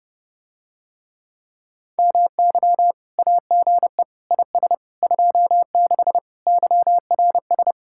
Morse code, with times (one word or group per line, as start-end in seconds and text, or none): 2.0-2.9 MY
3.2-4.0 AGE
4.3-4.7 IS
5.0-6.2 26
6.5-7.7 YRS